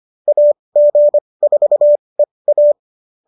Morse code, heard as a keyed tone, about 25 wpm, spaced as for 20 wpm, 570 Hz. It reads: AG4EA